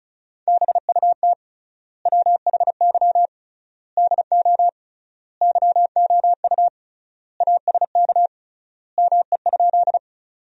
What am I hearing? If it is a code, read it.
Morse code, heard as a keyed tone, about 34 wpm, 700 Hz. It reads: BUT WHY DO YOU ASK ME?
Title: 3x Word-Spacing - Sentences from Top 200 Words - 35wpm